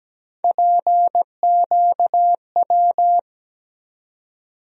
Morse code, heard as a keyed tone, about 17 wpm, 700 Hz. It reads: PQW
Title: USA Call Sign Postfixes - 3 Letters 17wpm